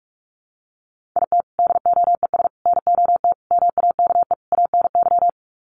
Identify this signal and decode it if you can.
Morse code, heard as a keyed tone, about 44 wpm, 700 Hz. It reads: IT DOES NOT MAKE ANY
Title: Sets of 5 Words Encore2 45wpm